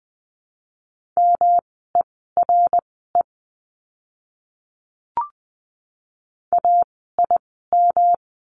Morse code, heard as a keyed tone, about 20 wpm, spaced as for 14 wpm, 700 Hz. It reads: MERE AIM
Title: 108 Lesson 18f - Letter m - Words 20wpm